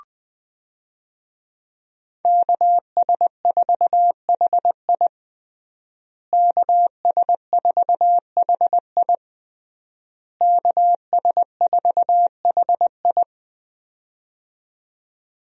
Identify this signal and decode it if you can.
Morse code, heard as a keyed tone, about 21 wpm, 700 Hz. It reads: KS4HI KS4HI KS4HI
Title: USA Call Signs - 5 Characters - Repeated 3x - 20wpm